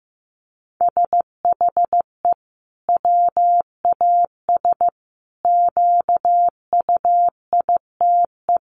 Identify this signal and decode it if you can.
Morse code, heard as a keyed tone, about 14 wpm, 700 Hz. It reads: SHE WAS QUITE
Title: Sentences from Top 1500 Words 15wpm